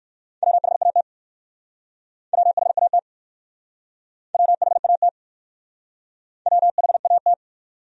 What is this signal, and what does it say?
Morse code, heard as a keyed tone, about 60 wpm, 700 Hz. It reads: WHAT WHAT WHAT WHAT